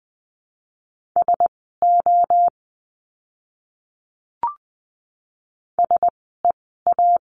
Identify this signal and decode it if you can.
Morse code, heard as a keyed tone, about 20 wpm, spaced as for 14 wpm, 700 Hz. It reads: SO SEA
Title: 068 Lesson 12f - Letter d - Words 20wpm